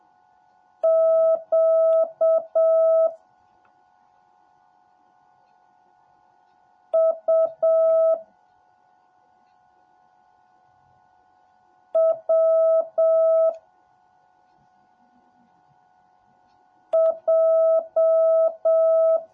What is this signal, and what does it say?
Morse code, heard as a keyed tone, about 7 wpm, 650 Hz. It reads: Q U W J